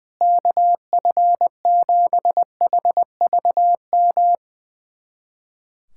Morse code, heard as a keyed tone, about 20 wpm, 700 Hz. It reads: KF7HVM
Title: USA Callsign DE Callsign 20wpm